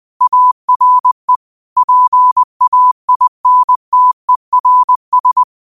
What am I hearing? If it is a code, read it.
Morse code, heard as a keyed tone, about 20 wpm, 1000 Hz. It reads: ARE PAINTERS